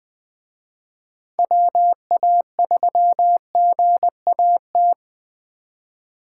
Morse code, heard as a keyed tone, about 20 wpm, 700 Hz. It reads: WA3GAT
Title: Speed Racing - USA 2x3 Call Signs 17wpm